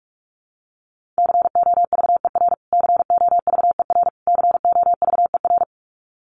Morse code, heard as a keyed tone, about 44 wpm, 700 Hz. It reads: COVER COVER COVER
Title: Top 500 Words Repeated 45wpm